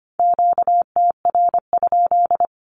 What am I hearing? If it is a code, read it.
Morse code, heard as a keyed tone, about 25 wpm, 700 Hz. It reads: QTR?
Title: Q-Codes Repeated 3x 25wpm